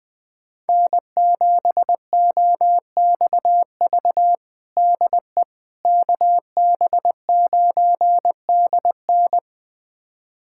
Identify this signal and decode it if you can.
Morse code, heard as a keyed tone, about 20 wpm, 700 Hz. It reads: N7OXV DE KB9DN
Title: USA Callsign DE Callsign - 5 Characters - 20wpm